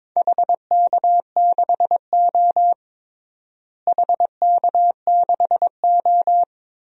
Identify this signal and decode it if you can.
Morse code, heard as a keyed tone, about 22 wpm, 700 Hz. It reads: HK6O HK6O
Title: DXCC-100 Call Signs - 4 Characters - Repeated 3x - 22wpm